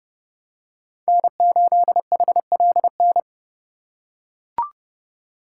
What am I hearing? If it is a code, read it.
Morse code, heard as a keyed tone, about 30 wpm, 700 Hz. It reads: N8HLN
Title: USA Call Signs - 5 Characters - 2x Character Spacing - 30wpm